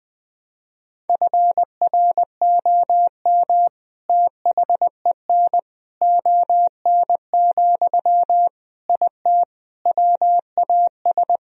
Morse code, heard as a keyed tone, about 20 wpm, 700 Hz. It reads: FROM THEN ON, IT WAS
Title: Sentences from Top 1000 Words - Review Part 2 - 20wpm